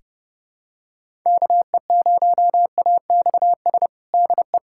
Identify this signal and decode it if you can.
Morse code, heard as a keyed tone, about 32 wpm, 700 Hz. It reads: KE0AXS DE
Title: USA Callsign DE Callsign 30wpm